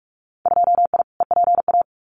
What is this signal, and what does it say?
Morse code, heard as a keyed tone, about 46 wpm, 700 Hz. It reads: 2I EPA
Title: Winter Field Day Exchanges 45wpm